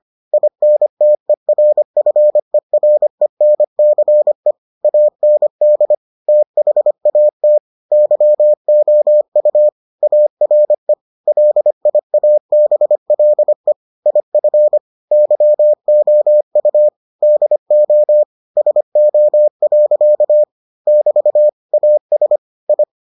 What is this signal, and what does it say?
Morse code, heard as a keyed tone, about 25 wpm, 600 Hz. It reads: INTERFERENCE AND THAT YOU ARE LIABLE IF YOU DO SO. <BT> AS I